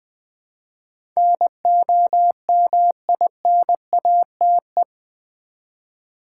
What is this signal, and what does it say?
Morse code, heard as a keyed tone, about 20 wpm, 700 Hz. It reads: NOMINATE
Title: Build A Word - 3 Letters At A Time - 8-Letter Words - 20wpm